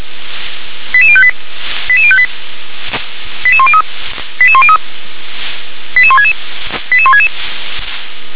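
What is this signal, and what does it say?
Five-tone selective call (ZVEI), heard as quick runs of five tones, tones of about 70 ms. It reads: 88058 88058 88103 88103 88166 88166
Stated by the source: ZVEI five-tone call 88006 received by SDR.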